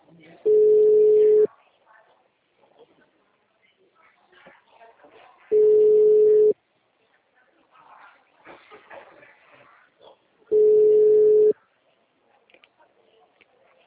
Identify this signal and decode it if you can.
Ringback tone (Europe, 425 Hz), with speech.